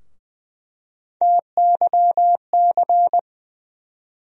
Morse code, heard as a keyed tone, about 20 wpm, 700 Hz. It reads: TYC
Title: USA Call Sign Postfixes - 3 Letters 20wpm